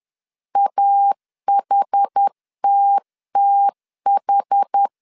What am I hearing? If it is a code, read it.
Morse code, heard as a keyed tone, about 10 wpm, 800 Hz. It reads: AHTTH
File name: 10wpm